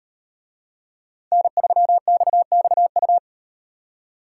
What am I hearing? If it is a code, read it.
Morse code, heard as a keyed tone, about 38 wpm, 700 Hz. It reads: N3XXU